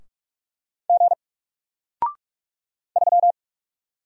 Morse code, heard as a keyed tone, about 44 wpm, 700 Hz. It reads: G 3
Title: Single Character (Letter:Number:Punctuation:Prosign) 45wpm